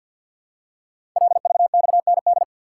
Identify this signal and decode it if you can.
Morse code, heard as a keyed tone, about 53 wpm, 700 Hz. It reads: LVXND
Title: Code Groups - 5 Letters - 50wpm